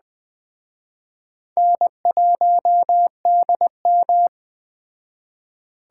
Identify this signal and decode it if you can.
Morse code, heard as a keyed tone, about 21 wpm, 700 Hz. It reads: N1DM
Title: Speed Racing - USA 1x2 Call Signs 17wpm